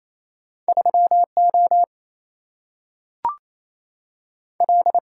Morse code, heard as a keyed tone, about 28 wpm, 700 Hz. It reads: 3O L
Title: Single-Letter Single-Number 28wpm